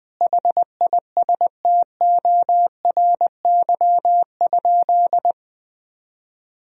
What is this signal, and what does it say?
Morse code, heard as a keyed tone, about 20 wpm, 700 Hz. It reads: HISTORY?
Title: Sentences from Top 1000 Words - Review Part 3 - 20wpm